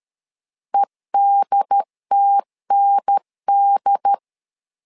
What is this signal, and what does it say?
Morse code, heard as a keyed tone, about 13 wpm, 800 Hz. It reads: EDTND
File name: LEZIONI / 3 MWRO / 3 MWR 12wpm 1711